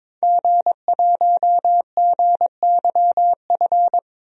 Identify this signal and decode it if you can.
Morse code, heard as a keyed tone, about 21 wpm, 700 Hz. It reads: G1GYF
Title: DXCC-100 Call Signs - 5 Characters - 22wpm